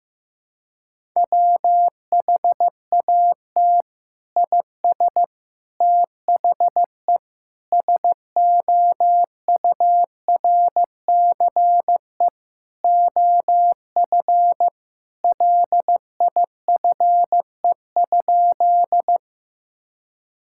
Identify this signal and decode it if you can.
Morse code, heard as a keyed tone, about 15 wpm, 700 Hz. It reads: WHAT IS THE SOURCE OF LIFE?
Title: Sentences from Top 1000 Words - Review Part 5 - 15wpm